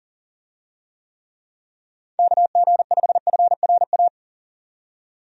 Morse code, heard as a keyed tone, about 40 wpm, 700 Hz. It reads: KG5FRA
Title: USA Call Signs - 6 Characters - 40wpm